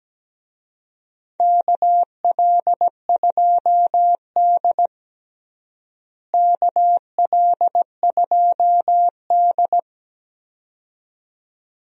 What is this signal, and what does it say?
Morse code, heard as a keyed tone, about 17 wpm, 700 Hz. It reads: KL2D KL2D
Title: USA Call Signs - 4 Characters - Repeated 2x - 17wpm